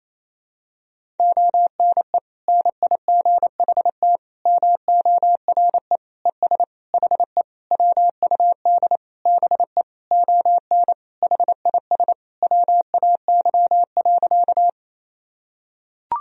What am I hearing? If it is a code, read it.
Morse code, heard as a keyed tone, about 27 wpm, 700 Hz. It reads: ONE NIGHT MORE ES HE WUD BE ON HIS WAY.
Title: Sentences from Top 300 Words and Abbreviations 28wpm